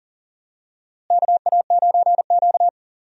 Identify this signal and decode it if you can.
Morse code, heard as a keyed tone, about 37 wpm, 700 Hz. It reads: KA9Q
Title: USA Call Signs - 4 Characters - Repeated 2x - 40wpm